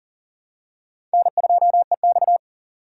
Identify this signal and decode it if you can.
Morse code, heard as a keyed tone, about 40 wpm, 700 Hz. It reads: N2EX
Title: USA Call Signs - 4 Characters - 4x Character Spacing - 40wpm